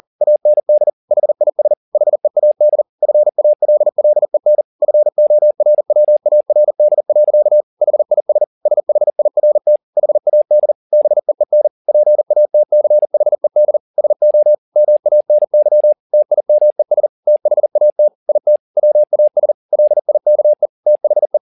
Morse code, heard as a keyed tone, about 40 wpm, 600 Hz. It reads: AND HIS HEAD FALLEN FORWARD. HIS SHIRT HAD BEEN PATCHED SO MANY TIMES THAT IT WAS LIKE THE